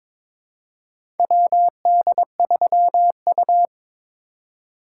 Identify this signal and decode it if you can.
Morse code, heard as a keyed tone, about 23 wpm, 700 Hz. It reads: WD3U
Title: USA Call Signs - 3x Character Spacing - 22wpm